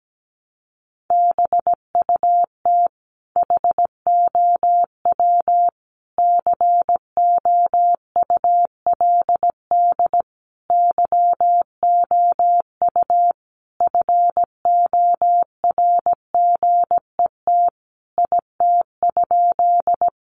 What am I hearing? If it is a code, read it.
Morse code, heard as a keyed tone, about 17 wpm, 700 Hz. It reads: BUT HOW COULD YOU FORGET IT?
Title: Sentences from Top 1200 Words 17wpm